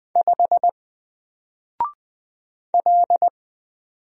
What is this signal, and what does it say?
Morse code, heard as a keyed tone, about 20 wpm, 700 Hz. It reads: 5 L
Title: Single Letter-Number 20wpm